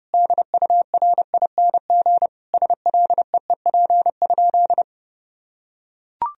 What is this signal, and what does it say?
Morse code, heard as a keyed tone, about 32 wpm, 700 Hz. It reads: DURING SLEEP?
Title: Sentences from Top 1000 Words - Review Part 2 - 30wpm